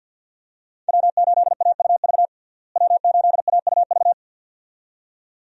Morse code, heard as a keyed tone, about 50 wpm, 700 Hz. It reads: W8AUV W8AUV
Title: USA Call Signs - 5 Characters - Repeated 2x - 50wpm